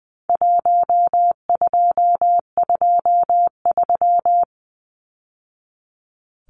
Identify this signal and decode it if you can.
Morse code, heard as a keyed tone, about 20 wpm, 700 Hz. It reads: 1223
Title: Numbers - Contest Style - 4-Digit 20wpm